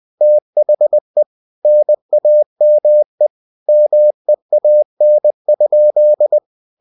Morse code, heard as a keyed tone, about 20 wpm, 600 Hz. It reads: THE NAME MEAN?
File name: cw-books / winnie-the-pooh / winnie-20wpm-page-0002